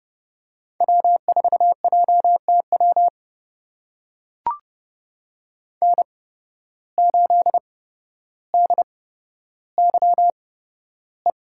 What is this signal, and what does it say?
Morse code, heard as a keyed tone, about 30 wpm, spaced as for 13 wpm, 700 Hz. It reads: W4JTW N8DYE
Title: USA Call Signs - 5 Characters - 8x Character Spacing - 30wpm